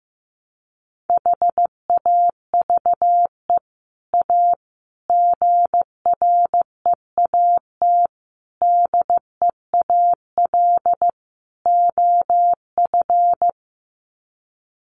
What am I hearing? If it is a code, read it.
Morse code, heard as a keyed tone, about 15 wpm, 700 Hz. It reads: HAVE A GREAT DEAL OF
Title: Sets of 5 Words Encore2 15wpm